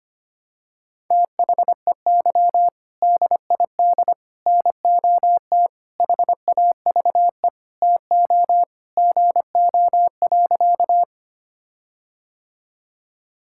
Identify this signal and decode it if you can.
Morse code, heard as a keyed tone, about 25 wpm, 700 Hz. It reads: THEY DID NOT HAVE TO GO.
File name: practice / 25wpm / Sentences with 4-Word Phrases - 25wpm